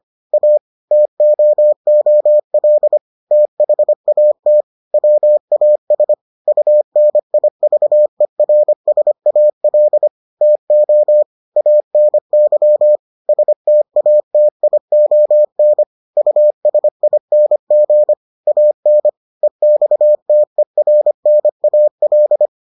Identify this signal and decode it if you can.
Morse code, heard as a keyed tone, about 25 wpm, 600 Hz. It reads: A TOOL THAT WAS UNIVERSAL TO ANY STATION USING AN EXTERNAL